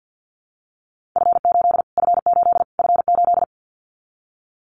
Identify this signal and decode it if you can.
Morse code, heard as a keyed tone, about 50 wpm, 700 Hz. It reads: F8 F8 F8
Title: Callsign Prefixes for Top 25 DXCC - Repeated 3x - 50wpm